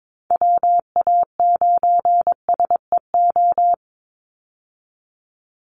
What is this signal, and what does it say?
Morse code, heard as a keyed tone, about 22 wpm, 700 Hz. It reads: WA9SEO
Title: Call Signs Repeated 2x - 22wpm